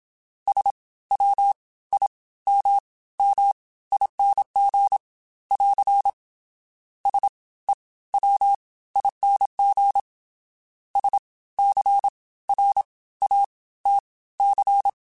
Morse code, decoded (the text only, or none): SWIMMING <AR> SEWING SCRATC